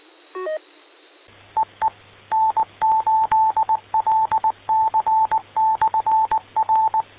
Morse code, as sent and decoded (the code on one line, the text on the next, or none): . / -. ---.. .-.. -.-. -..-. .-.
EE N8LC/R